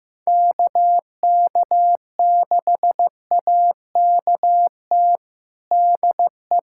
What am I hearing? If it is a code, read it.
Morse code, heard as a keyed tone, about 15 wpm, 700 Hz. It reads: KK6AKT DE
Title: USA Callsign DE Callsign - 6 Characters - 15wpm